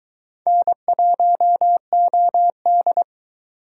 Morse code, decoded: N1OD

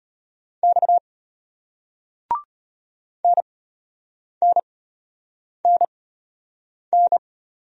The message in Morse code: X NNNN

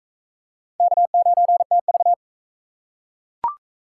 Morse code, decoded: K9TV

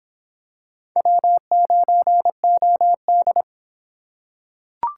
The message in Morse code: W9OD

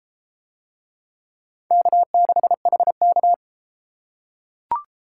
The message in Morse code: K6HK

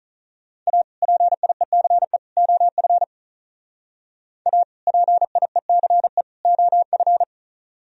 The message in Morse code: A PIECE OF A PIECE OF

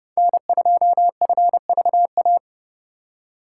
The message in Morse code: N2FVA

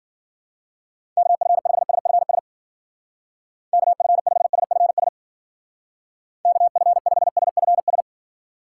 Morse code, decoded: KF5SFS KF5SFS KF5SFS

